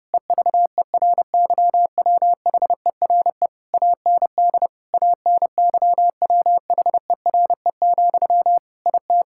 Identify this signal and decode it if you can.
Morse code, heard as a keyed tone, about 32 wpm, 700 Hz. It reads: EVERYWHERE AND ANYWHERE, IT
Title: Sentences from Top 1600 Words 30wpm